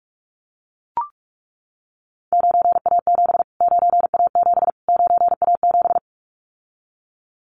Morse code, heard as a keyed tone, about 44 wpm, 700 Hz. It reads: 9A7 9A7 9A7
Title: Callsign Prefixes for Top 25 DXCC - Repeated 3x - 45wpm